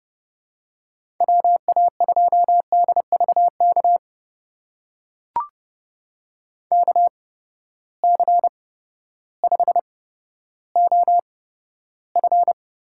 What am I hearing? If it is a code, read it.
Morse code, heard as a keyed tone, about 30 wpm, spaced as for 14 wpm, 700 Hz. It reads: WA2DVK KC5OF
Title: USA Call Signs - 6 Characters - 8x Character Spacing - 30wpm